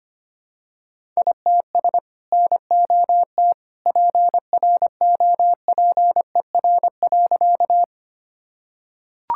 Morse code, heard as a keyed tone, about 25 wpm, 700 Hz. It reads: ITS NOT PROPER.
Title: Sentences from Top 1000 Words - Review Part 5 - 25wpm